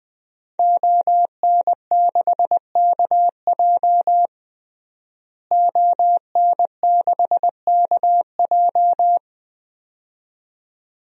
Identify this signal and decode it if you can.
Morse code, heard as a keyed tone, about 20 wpm, 700 Hz. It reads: ON6KJ ON6KJ
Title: DXCC-100 Call Signs - 5 Characters - Repeated 2x - 20wpm